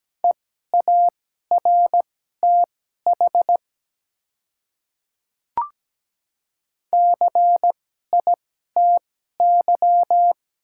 Morse code, heard as a keyed tone, about 17 wpm, spaced as for 12 wpm, 700 Hz. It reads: EARTH CITY
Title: 2x Character Spacing - Top 300 Words - 17wpm